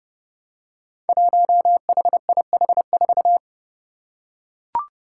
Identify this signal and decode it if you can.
Morse code, heard as a keyed tone, about 32 wpm, 700 Hz. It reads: 1HIH4
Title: Five Challenging Letter-Number Combinations 30wpm